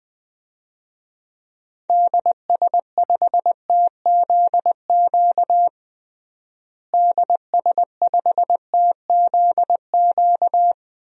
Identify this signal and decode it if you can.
Morse code, heard as a keyed tone, about 20 wpm, 700 Hz. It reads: DS5TZQ DS5TZQ
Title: DXCC-100 Call Signs - 6 Characters - Repeated 3x - 20wpm